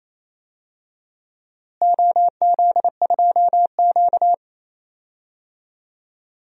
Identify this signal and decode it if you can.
Morse code, heard as a keyed tone, about 27 wpm, 700 Hz. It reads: OZ2Q
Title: DXCC-100 Call Signs - 4 Characters - Repeated 2x - 28wpm